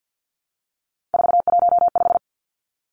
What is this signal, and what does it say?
Morse code, heard as a keyed tone, about 50 wpm, 700 Hz. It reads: 415
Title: Numbers - 3-Digit 50wpm